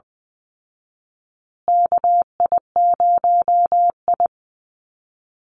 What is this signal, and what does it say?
Morse code, heard as a keyed tone, about 20 wpm, 700 Hz. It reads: KI0I
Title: Call Signs Repeated 2x - 20wpm